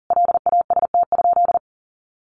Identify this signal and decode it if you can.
Morse code, heard as a keyed tone, about 40 wpm, 700 Hz. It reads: LAST?
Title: Sentences from Top 1300 Words 40wpm